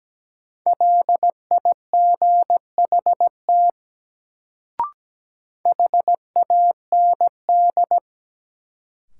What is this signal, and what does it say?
Morse code, heard as a keyed tone, about 17 wpm, 700 Hz. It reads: LIGHT HAND
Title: Sentences from Top 200 Words - New Word Drill - 17wpm